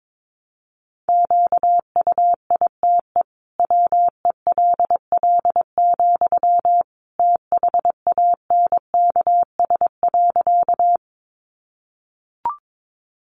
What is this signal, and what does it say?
Morse code, heard as a keyed tone, about 22 wpm, 700 Hz. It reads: QUITE WELL, THANKS.
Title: Sentences from Top 700 Words 22wpm